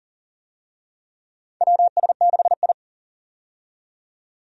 Morse code, heard as a keyed tone, about 40 wpm, 700 Hz. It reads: WS6I
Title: USA Call Signs - 4 Characters - 40wpm